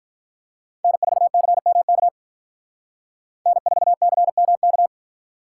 Morse code, heard as a keyed tone, about 46 wpm, 700 Hz. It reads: N4CMK N4CMK